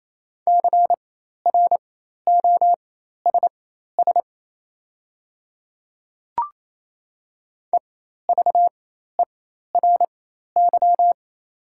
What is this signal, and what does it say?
Morse code, heard as a keyed tone, about 28 wpm, spaced as for 13 wpm, 690 Hz. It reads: CROSS EVERY
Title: 4x Character Spacing - Top 300 Words - 28wpm